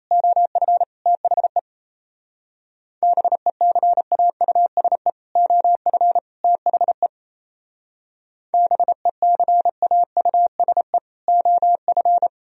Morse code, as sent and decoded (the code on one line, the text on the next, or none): --- ..-. / - .... / -... -.-. .- ..- ... / --- ..-. / - .... / -... -.-. .- ..- ... / --- ..-.
OF THE BECAUSE OF THE BECAUSE OF